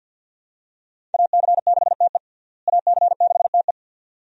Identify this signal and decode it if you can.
Morse code, heard as a keyed tone, about 53 wpm, 700 Hz. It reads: AC6TE AC6TE
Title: USA Call Signs - 5 Characters - Repeated 2x - 50wpm